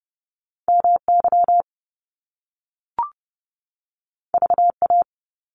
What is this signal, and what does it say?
Morse code, heard as a keyed tone, about 30 wpm, 700 Hz. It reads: MY VA
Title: Top Two Letter Combinations in Words 30wpm